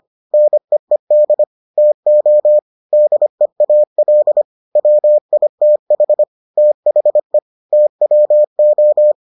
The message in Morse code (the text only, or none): NEED TO DEAL WITH THE TWO